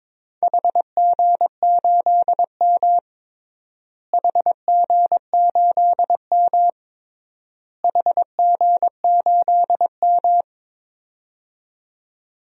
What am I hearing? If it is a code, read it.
Morse code, heard as a keyed tone, about 22 wpm, 700 Hz. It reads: HG8M HG8M HG8M
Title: DXCC-100 Call Signs - 4 Characters - Repeated 3x - 22wpm